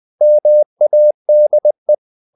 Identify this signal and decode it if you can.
Morse code, heard as a keyed tone, about 20 wpm, 600 Hz. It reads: MADE
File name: the old man and the sea / the old man and the sea part 1 20wpm